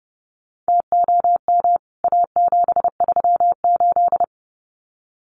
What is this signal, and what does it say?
Morse code, heard as a keyed tone, about 30 wpm, 700 Hz. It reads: TOM A738
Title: CWT Exchange 30wpm